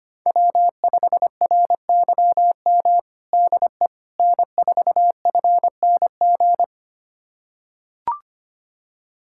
Morse code, heard as a keyed tone, about 25 wpm, 700 Hz. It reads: W5RYM DE N4FNG
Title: USA Callsign DE Callsign - 5 Characters - 25wpm